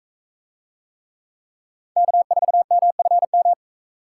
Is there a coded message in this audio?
Yes: Morse code from 2.0 to 3.5 s, heard as a keyed tone, about 39 wpm, 700 Hz. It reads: K4MFM